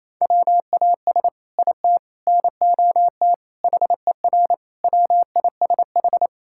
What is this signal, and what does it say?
Morse code, heard as a keyed tone, about 28 wpm, 700 Hz. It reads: WAS IT NOT HER WISH
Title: Sentences from Top 1600 Words 28wpm